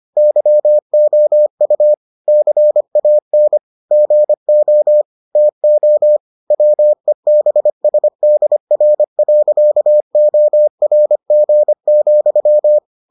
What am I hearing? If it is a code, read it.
Morse code, heard as a keyed tone, about 25 wpm, 600 Hz. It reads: YOU CAN GO TO WEBSDR.ORG,